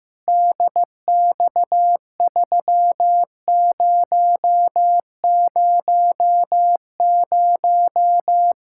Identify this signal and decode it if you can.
Morse code, heard as a keyed tone, about 16 wpm, 700 Hz. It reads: DX3000
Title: K1USN SST QSO Practice 15wpm